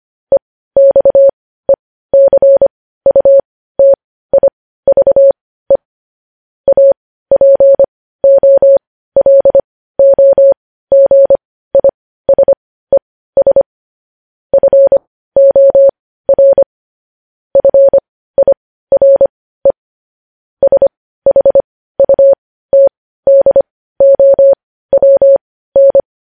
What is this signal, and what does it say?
Morse code, heard as a keyed tone, about 25 wpm, spaced as for 15 wpm, 550 Hz. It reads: EXECUTIVE APOLOGISES FOR FIRE SHUTDOWN
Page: news headlines in morse code at 15 wpm